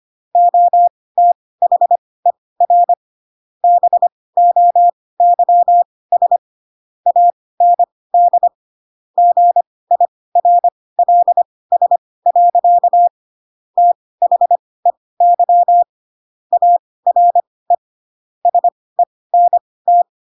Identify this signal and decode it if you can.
Morse code, heard as a keyed tone, about 25 wpm, spaced as for 18 wpm, 700 Hz. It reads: OTHER BOYS AND GIRLS. THEY ARE SENT
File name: Pinocchio / Pinocchio ch4 12 25 ik1hge